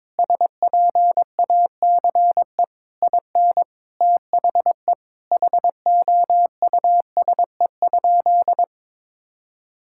Morse code, heard as a keyed tone, about 22 wpm, 700 Hz. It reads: SPACE IN THE HOUSE?
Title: Sentences from Top 1000 Words - Review Part 5 - 22wpm